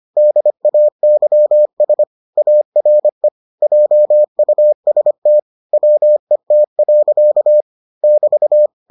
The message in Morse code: DAYS ARE JUST WET. <BT>

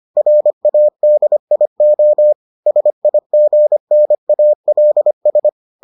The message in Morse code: RADIO SIGNALS

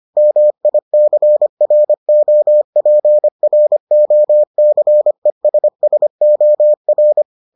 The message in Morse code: MICROPROCESSOR

